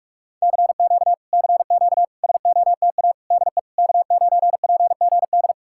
Morse code, heard as a keyed tone, about 40 wpm, 700 Hz. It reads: CQ CQ SOTA DE K9PGD